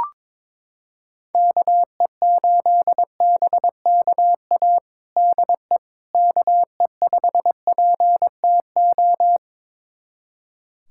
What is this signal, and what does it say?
Morse code, heard as a keyed tone, about 22 wpm, 700 Hz. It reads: KE8BKA DE KE5PTO
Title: USA Callsign DE Callsign - 6 Characters - 22wpm